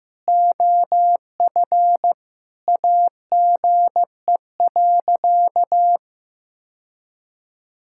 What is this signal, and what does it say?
Morse code, heard as a keyed tone, about 15 wpm, 700 Hz. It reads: OF AGE.
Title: Sentences from Top 1000 Words - Review Part 6 - 15wpm